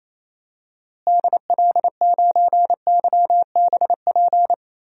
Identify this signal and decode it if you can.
Morse code, heard as a keyed tone, about 28 wpm, 700 Hz. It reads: DL9YBP
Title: DXCC-100 Call Signs - 6 Characters - Repeated 2x - 28wpm